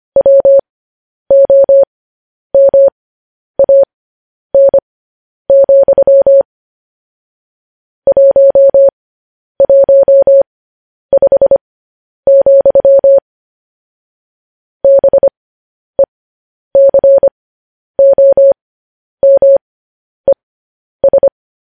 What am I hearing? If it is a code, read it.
Morse code, heard as a keyed tone, about 25 wpm, spaced as for 10 wpm, 550 Hz. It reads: WOMAN, 115, BECOMES
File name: podcast / News headlines-1x-25x10